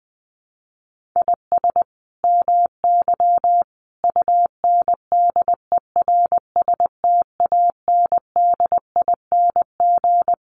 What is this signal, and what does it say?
Morse code, heard as a keyed tone, about 20 wpm, 700 Hz. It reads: IS MY UNDERSTANDING
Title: Sentences from Top 1400 Words 20wpm